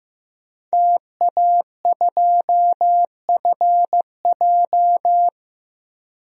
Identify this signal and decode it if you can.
Morse code, heard as a keyed tone, about 15 wpm, 700 Hz. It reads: TA2FJ
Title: DXCC-100 Call Signs - 5 Characters - 15wpm